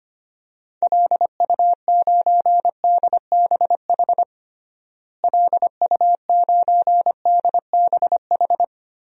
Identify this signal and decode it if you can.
Morse code, heard as a keyed tone, about 25 wpm, 700 Hz. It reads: LU9DBH LU9DBH
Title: DXCC-100 Call Signs - 6 Characters - Repeated 2x - 25wpm